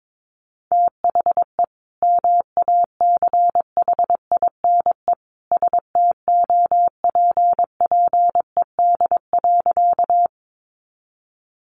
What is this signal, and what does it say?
Morse code, heard as a keyed tone, about 22 wpm, 700 Hz. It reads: THE MACHINE STOPPED.
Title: A Sound Of Thunder 22wpm